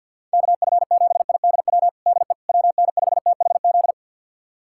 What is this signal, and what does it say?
Morse code, heard as a keyed tone, about 48 wpm, 700 Hz. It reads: KF7IDW DE WN5THZ
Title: USA Callsign DE Callsign - 6 Characters - 50wpm